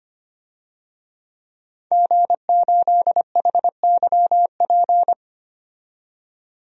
Morse code, heard as a keyed tone, about 25 wpm, 700 Hz. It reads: G8HYP
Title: DXCC-100 Call Signs - 5 Characters - 25wpm